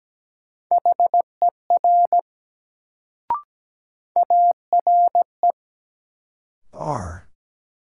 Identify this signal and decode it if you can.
Morse code, heard as a keyed tone, about 17 wpm, 700 Hz. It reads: HER ARE
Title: Sentences from Top 100 Words - New Word Drill - 17wpm